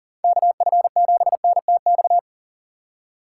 Morse code, heard as a keyed tone, about 37 wpm, 700 Hz. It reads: KF7NTX
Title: USA Call Signs - 6 Characters - 3x Character Spacing - 40wpm